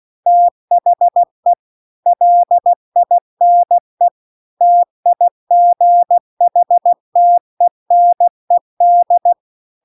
Morse code, heard as a keyed tone, about 16 wpm, 700 Hz. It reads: THE LINE TIGHTENED